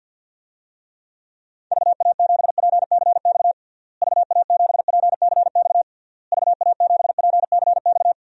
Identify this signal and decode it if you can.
Morse code, heard as a keyed tone, about 52 wpm, 700 Hz. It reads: VA7PCX VA7PCX VA7PCX
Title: DXCC-100 Call Signs - 6 Characters - Repeated 3x - 50wpm